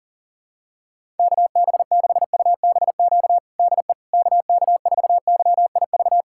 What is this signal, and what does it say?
Morse code, heard as a keyed tone, about 40 wpm, 700 Hz. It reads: KB6UBQ DE KK4YIV
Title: USA Callsign DE Callsign - 6 Characters - 40wpm